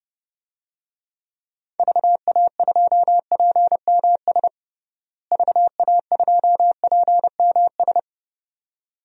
Morse code, heard as a keyed tone, about 31 wpm, 700 Hz. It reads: VA2PMS VA2PMS
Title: DXCC-100 Call Signs - 6 Characters - Repeated 2x - 30wpm